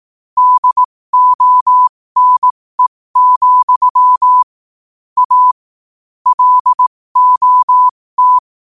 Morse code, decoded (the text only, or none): DONE, A LOT